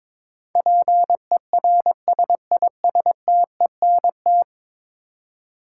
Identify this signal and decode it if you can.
Morse code, heard as a keyed tone, about 22 wpm, 700 Hz. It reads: PERSISTENT
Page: morse-code-ninja-practice-22wpm